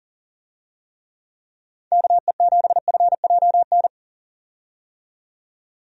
Morse code, heard as a keyed tone, about 39 wpm, 700 Hz. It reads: KE7FJN
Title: USA Call Signs - 6 Characters - 40wpm